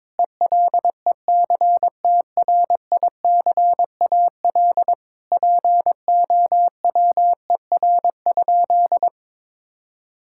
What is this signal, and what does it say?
Morse code, heard as a keyed tone, about 22 wpm, 700 Hz. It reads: ELECTRICAL POWER?
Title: Sentences from Top 1000 Words - Review Part 6 - 22wpm